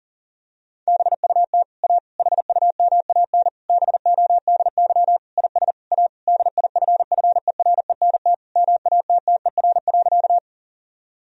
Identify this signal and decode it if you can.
Morse code, heard as a keyed tone, about 40 wpm, 700 Hz. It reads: BUT A HUMAN BODY IS A DIFFERENT MATTER.
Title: Sentences from Top 1000 Words - Review Part 3 - 40wpm